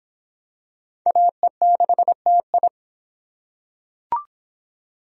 Morse code, heard as a keyed tone, about 22 wpm, 700 Hz. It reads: AE6TI